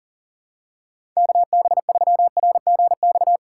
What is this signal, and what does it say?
Morse code, heard as a keyed tone, about 40 wpm, 700 Hz. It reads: KB3RGX
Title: USA Call Signs - 6 Characters - 2x Character Spacing - 40wpm